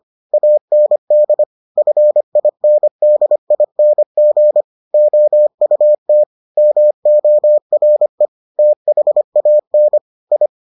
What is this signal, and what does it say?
Morse code, heard as a keyed tone, about 25 wpm, 600 Hz. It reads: AND FINDING OUT MORE THAN I